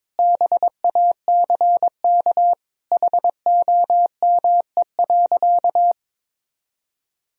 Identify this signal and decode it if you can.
Morse code, heard as a keyed tone, about 22 wpm, 700 Hz. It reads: BACK HOME.
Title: Sentences from Top 500 Words and Abbreviations 22wpm